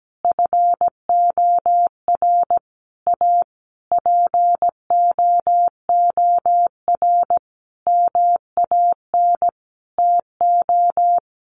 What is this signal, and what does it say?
Morse code, heard as a keyed tone, about 17 wpm, 700 Hz. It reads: FOR A POOR MAN TO
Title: Sentences from Top 1300 Words 17wpm